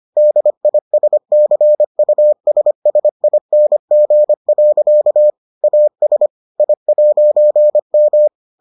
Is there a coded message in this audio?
Yes: Morse code from 0.2 to 8.3 s, heard as a keyed tone, about 25 wpm, 600 Hz. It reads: DISCUSSING. AS I'M